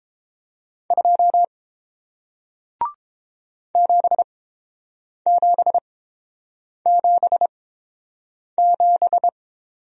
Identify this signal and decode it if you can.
Morse code, heard as a keyed tone, about 30 wpm, spaced as for 7 wpm, 700 Hz. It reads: 2 7777